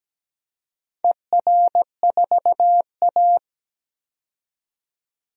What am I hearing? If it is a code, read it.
Morse code, heard as a keyed tone, about 17 wpm, 700 Hz. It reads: ER4A
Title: DXCC-100 Call Signs - 4 Characters - 17wpm